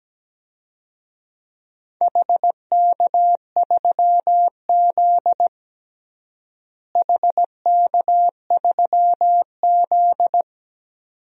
Morse code, heard as a keyed tone, about 17 wpm, 700 Hz. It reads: HK3Z HK3Z
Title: DXCC-100 Call Signs - 4 Characters - Repeated 2x - 17wpm